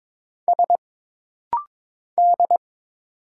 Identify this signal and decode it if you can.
Morse code, heard as a keyed tone, about 22 wpm, 700 Hz. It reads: S D
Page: This rapid-fire timing is geared to push your ICR (Instant Character Recognition) skills.